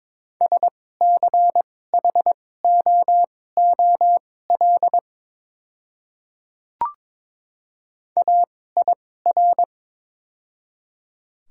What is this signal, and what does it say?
Morse code, heard as a keyed tone, about 22 wpm, spaced as for 16 wpm, 700 Hz. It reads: SCHOOL AIR
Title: 2x Character Spacing - Top 300 Words - 22wpm